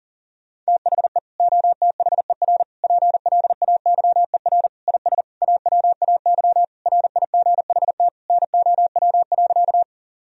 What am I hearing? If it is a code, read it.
Morse code, heard as a keyed tone, about 37 wpm, 700 Hz. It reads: THE OTHER PLAYER IS AWAY RIGHT NOW.
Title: Sentences from Top 1000 Words - Review Part 4 - 40wpm